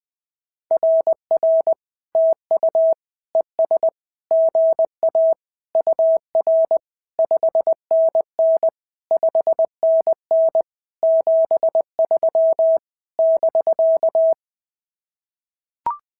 Morse code, heard as a keyed tone, about 19 wpm, 650 Hz. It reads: RR TU ES GA UR 5NN 5NN 73 <BK>
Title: SOTA QSO Practice 20wpm